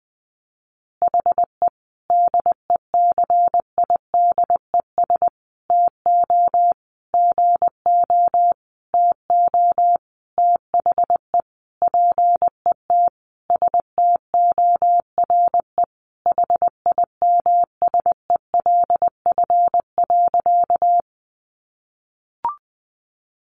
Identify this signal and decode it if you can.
Morse code, heard as a keyed tone, about 20 wpm, 700 Hz. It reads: HE DECIDES TO GO TO THE PET STORE HIMSELF.
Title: Additional Easy Phrases 20wpm